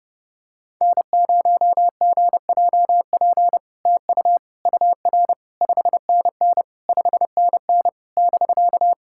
Morse code, heard as a keyed tone, about 30 wpm, 700 Hz. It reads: N0GJP TU UR 5NN 5NN <BK>
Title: SOTA QSO Practice 30wpm